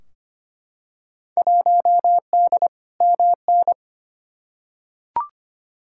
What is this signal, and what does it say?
Morse code, heard as a keyed tone, about 25 wpm, 700 Hz. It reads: 1D MN
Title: Field Day Exchanges - 25wpm